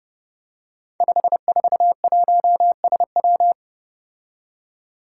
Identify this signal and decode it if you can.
Morse code, heard as a keyed tone, about 30 wpm, 700 Hz. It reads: 541SW
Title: Five Challenging Letter-Number Combinations 30wpm